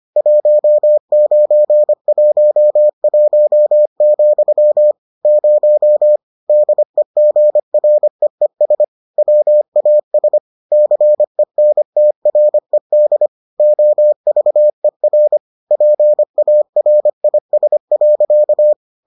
Morse code, heard as a keyed tone, about 25 wpm, 600 Hz. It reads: 1911, 0 DEGREES WAS CENTRED OVER PARIS.